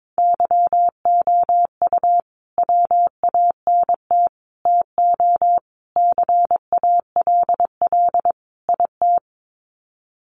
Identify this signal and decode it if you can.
Morse code, heard as a keyed tone, about 22 wpm, 700 Hz. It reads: YOU WANT TO CALL IT
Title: Sets of 5 Words Encore2 22wpm